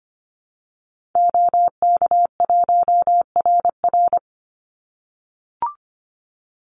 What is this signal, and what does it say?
Morse code, heard as a keyed tone, about 25 wpm, 700 Hz. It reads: OK1RR
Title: Call Signs - 25wpm